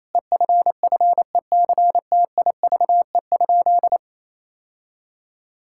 Morse code, heard as a keyed tone, about 27 wpm, 700 Hz. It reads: EFFECTIVE?
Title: Sentences from Top 1000 Words - Review Part 6 - 28wpm